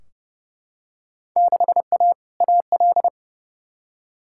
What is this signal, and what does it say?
Morse code, heard as a keyed tone, about 32 wpm, 700 Hz. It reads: 6A AL
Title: Field Day Exchanges Encore - 30wpm